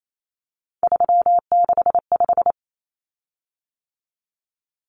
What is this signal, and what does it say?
Morse code, heard as a keyed tone, about 28 wpm, 700 Hz. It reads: 365
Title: Common Cut Numbers - 3-Digit 28wpm